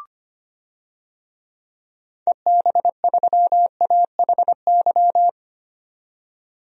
Morse code, heard as a keyed tone, about 25 wpm, 700 Hz. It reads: EB3AHY
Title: DXCC-100 Call Signs - 6 Characters - 25wpm